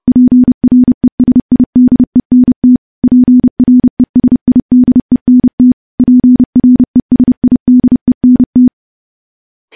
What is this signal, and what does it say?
Morse code, heard as a keyed tone, about 31 wpm, 250 Hz. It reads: PRESIDENT PRESIDENT PRESIDENT
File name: morse / morse-training-30WPM